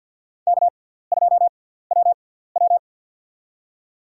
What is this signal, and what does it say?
Morse code, heard as a keyed tone, about 51 wpm, spaced as for 17 wpm, 700 Hz. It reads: K2WW
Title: USA Call Signs - 4 Characters - 6x Character Spacing - 50wpm